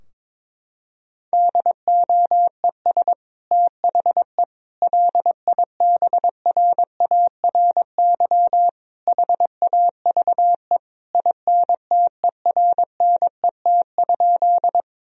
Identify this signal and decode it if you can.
Morse code, heard as a keyed tone, about 22 wpm, 700 Hz. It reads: DOES THE LIBRARY HAVE INTERNET?
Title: Sentences from Top 1000 Words - Review Part 4 - 22wpm